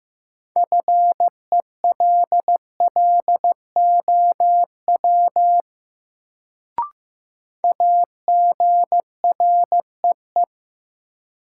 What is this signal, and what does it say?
Morse code, heard as a keyed tone, about 15 wpm, 700 Hz. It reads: FELLOW AGREE
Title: Sets of 5 Words Encore - New Word Drill - 15wpm